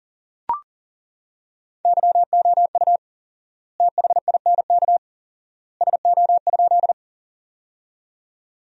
Morse code, heard as a keyed tone, about 40 wpm, 700 Hz. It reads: YOU THINK SO?
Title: 4x Word-Spacing - Sentences from Top 200 Words - 40wpm